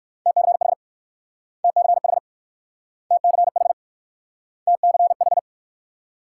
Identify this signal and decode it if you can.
Morse code, heard as a keyed tone, about 58 wpm, 690 Hz. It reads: TCH TCH TCH TCH